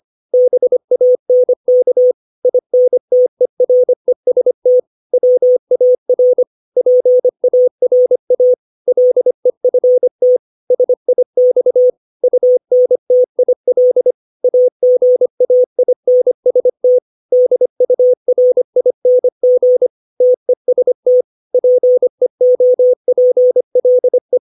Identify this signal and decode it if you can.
Morse code, heard as a keyed tone, about 25 wpm, 500 Hz. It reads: BANK INTEREST WAR PARA LEFT SIX UNTIL AGAINST DURING TEST PEOPLE
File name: 632 words / 632 Words 25WPM